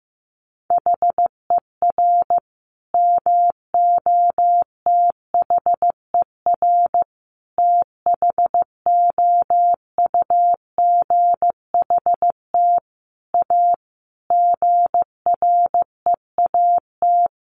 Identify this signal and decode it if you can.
Morse code, heard as a keyed tone, about 15 wpm, 700 Hz. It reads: HER MOTHER THOUGHT A GREAT
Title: Sentences from Top 800 Words 15wpm